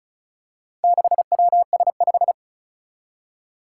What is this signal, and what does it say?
Morse code, heard as a keyed tone, about 36 wpm, 700 Hz. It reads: 6WS5
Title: Four Challenging Letter-Number Combinations 35wpm